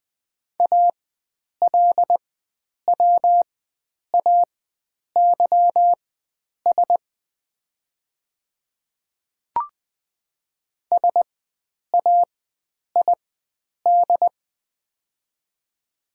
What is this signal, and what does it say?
Morse code, heard as a keyed tone, about 20 wpm, spaced as for 9 wpm, 700 Hz. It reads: ALWAYS SAID
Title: 4x Character Spacing - Top 300 Words - 20wpm